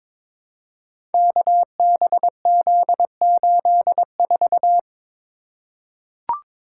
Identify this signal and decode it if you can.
Morse code, heard as a keyed tone, about 22 wpm, 700 Hz. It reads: KBZ84
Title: Code Groups - 5 Letter:Numbers - 22wpm